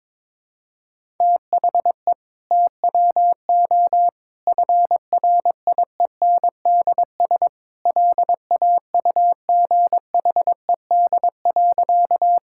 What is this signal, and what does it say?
Morse code, heard as a keyed tone, about 22 wpm, 700 Hz. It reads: THE TWO FRIENDS LAUGHED.
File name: practice / 22wpm / Sentences with 2-Word Phrases - 22wpm